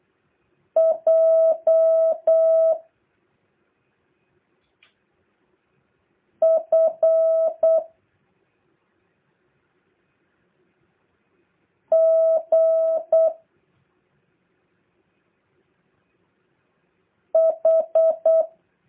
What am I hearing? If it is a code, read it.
Morse code, heard as a keyed tone, about 8 wpm, 650 Hz. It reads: J F G H
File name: jh8cbh / 2023cw